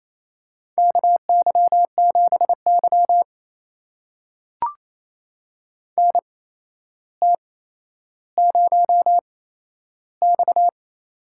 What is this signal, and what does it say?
Morse code, heard as a keyed tone, about 28 wpm, spaced as for 12 wpm, 700 Hz. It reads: KY7Y NT0X
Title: USA Call Signs - 8x Character Spacing - 28wpm